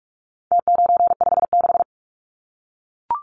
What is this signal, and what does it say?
Morse code, heard as a keyed tone, about 44 wpm, 700 Hz. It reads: T956
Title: Numbers - Contest Style - 4-Digit 45wpm